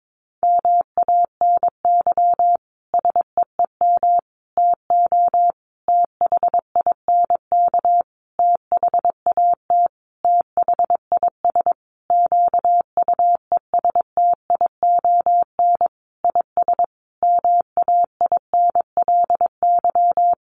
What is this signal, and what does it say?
Morse code, heard as a keyed tone, about 22 wpm, 700 Hz. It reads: MANY SEEM TO THINK THAT THIS QUESTION IS MAINLY
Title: Sentences from Top 1400 Words 22wpm